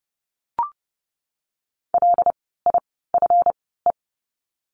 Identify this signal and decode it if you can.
Morse code, heard as a keyed tone, about 30 wpm, spaced as for 17 wpm, 700 Hz. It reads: LIFE
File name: course / 30:10wpm / 230 Lesson 37f - Letter x - Words 30wpm